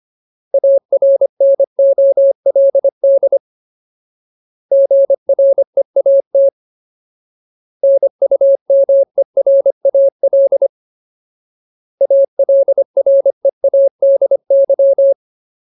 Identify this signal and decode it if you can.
Morse code, heard as a keyed tone, about 25 wpm, 550 Hz. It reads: ARNOLD GREAT NUMERAL ALREADY
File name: Ultimate Word List / 25WPM Ultimate